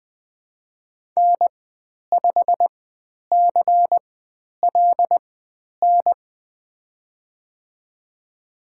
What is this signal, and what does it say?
Morse code, heard as a keyed tone, about 20 wpm, spaced as for 10 wpm, 700 Hz. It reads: N5CLN
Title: 095 Lesson 16g - Letter c - Call Signs 20wpm